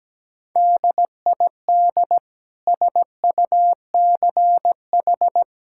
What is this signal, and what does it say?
Morse code, heard as a keyed tone, about 17 wpm, 700 Hz. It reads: DID SUCH
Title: Sentences from Top 1000 Words - Review Part 5 - 17wpm